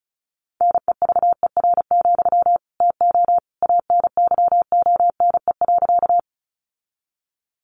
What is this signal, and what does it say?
Morse code, heard as a keyed tone, about 34 wpm, 700 Hz. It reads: NEVER, TO ANYONE.
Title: Sentences from Top 900 Words 35wpm